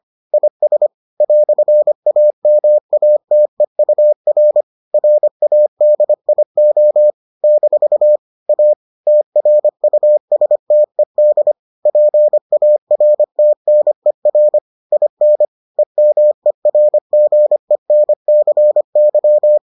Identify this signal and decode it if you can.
Morse code, heard as a keyed tone, about 25 wpm, 600 Hz. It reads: IS "AMATEUR RADIO - A TRUSTED PARTNER IN EMERGENCY